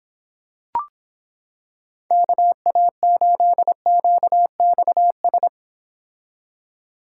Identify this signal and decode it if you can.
Morse code, heard as a keyed tone, about 26 wpm, 700 Hz. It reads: KA8QXS